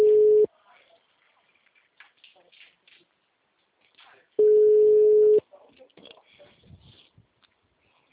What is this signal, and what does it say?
Ringback tone (Europe, 425 Hz), with speech.
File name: out- -2023- - - .2